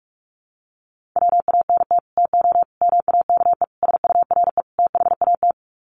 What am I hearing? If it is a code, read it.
Morse code, heard as a keyed tone, about 44 wpm, 700 Hz. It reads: WANT TO MAKE SURE THAT
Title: Sets of 5 Words 45wpm